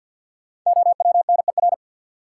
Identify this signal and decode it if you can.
Morse code, heard as a keyed tone, about 49 wpm, 700 Hz. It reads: OWNER
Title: Sentences from Top 1600 Words - New Word Drill - 50wpm